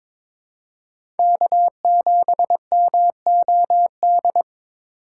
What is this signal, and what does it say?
Morse code, heard as a keyed tone, about 22 wpm, 700 Hz. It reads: K7MOD
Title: USA Call Signs - 5 Characters - 6x Character Spacing - 22wpm